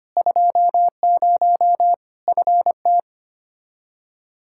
Morse code, heard as a keyed tone, about 25 wpm, 700 Hz. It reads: 20 FT
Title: K1USN SST QSO Practice 25wpm